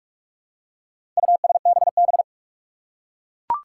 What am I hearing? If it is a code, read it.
Morse code, heard as a keyed tone, about 41 wpm, 700 Hz. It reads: USBB
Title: Four Challenging Letter-Number Combinations 45wpm